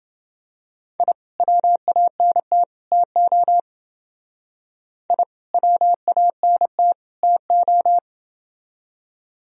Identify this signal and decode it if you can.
Morse code, heard as a keyed tone, about 29 wpm, 700 Hz. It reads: I WANT TO I WANT TO